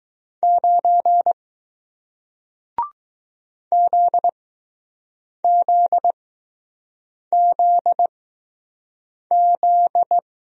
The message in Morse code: 9 ZZZZ

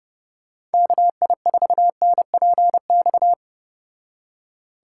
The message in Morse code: KI4NPX

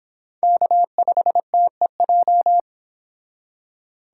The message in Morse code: K5TEJ

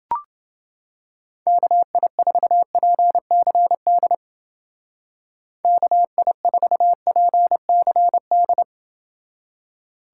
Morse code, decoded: KI4PCD KI4PCD

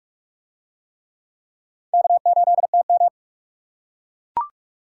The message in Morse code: K8TM

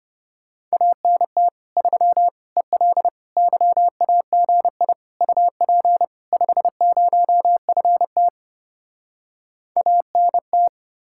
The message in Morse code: ANT 3 EL YAGI UP 50FT ANT

